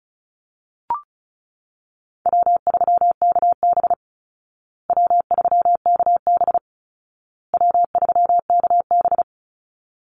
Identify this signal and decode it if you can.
Morse code, heard as a keyed tone, about 34 wpm, 700 Hz. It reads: W3KB W3KB W3KB
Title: Easy Call Signs Repeated 3x - 35wpm